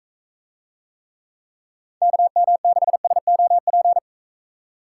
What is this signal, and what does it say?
Morse code, heard as a keyed tone, about 44 wpm, 700 Hz. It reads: KM6SOP